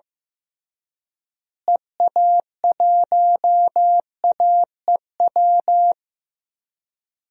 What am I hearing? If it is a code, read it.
Morse code, heard as a keyed tone, about 15 wpm, 700 Hz. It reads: EA1AEW
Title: DXCC-100 Call Signs - 6 Characters - Repeated 3x - 15wpm